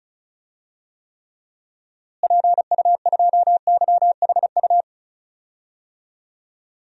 Morse code, heard as a keyed tone, about 36 wpm, 700 Hz. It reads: PU2YHU
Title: DXCC-100 Call Signs - 6 Characters - Repeated 2x - 35wpm